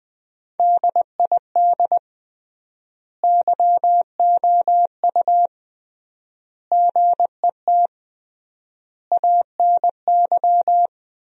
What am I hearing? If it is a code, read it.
Morse code, heard as a keyed tone, about 20 wpm, 700 Hz. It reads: DID YOU GET ANY
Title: 3x Word-Spacing - Sentences from Top 400 Words - 20wpm